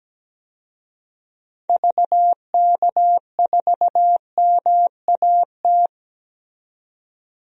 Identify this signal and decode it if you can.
Morse code, heard as a keyed tone, about 17 wpm, 700 Hz. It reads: VK4MAT
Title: DXCC-100 Call Signs - 6 Characters - 17wpm